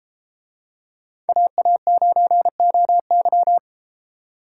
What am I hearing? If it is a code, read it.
Morse code, heard as a keyed tone, about 31 wpm, 700 Hz. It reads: AA9OY